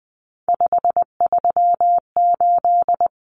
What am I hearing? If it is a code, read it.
Morse code, heard as a keyed tone, about 20 wpm, 700 Hz. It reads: 538
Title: Numbers - 3-Digit 20wpm